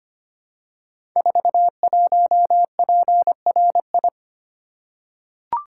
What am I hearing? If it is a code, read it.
Morse code, heard as a keyed tone, about 25 wpm, 700 Hz. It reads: 41PRI
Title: Code Groups - 5 Letter:Numbers - 25wpm